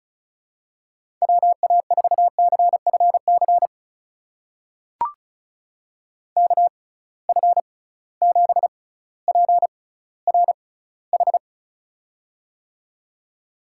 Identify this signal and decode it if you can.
Morse code, heard as a keyed tone, about 34 wpm, spaced as for 18 wpm, 700 Hz. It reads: WA4CFC KF7PRH
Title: USA Call Signs - 6 Characters - 6x Character Spacing - 35wpm